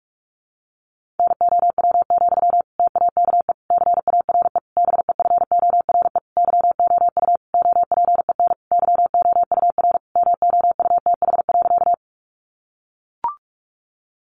Morse code, heard as a keyed tone, about 44 wpm, 700 Hz. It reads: NOW, TAKE CARE BEFORE YOU OPEN YOUR MOUTH.
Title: Sentences from Top 1500 Words 45wpm